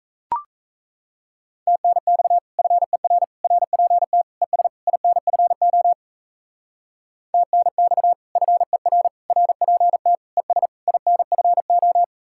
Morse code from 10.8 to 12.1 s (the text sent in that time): INFO